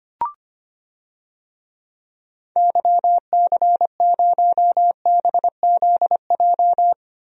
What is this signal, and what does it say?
Morse code, heard as a keyed tone, about 25 wpm, 700 Hz. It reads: YC0BZJ
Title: DXCC-100 Call Signs - 6 Characters - 25wpm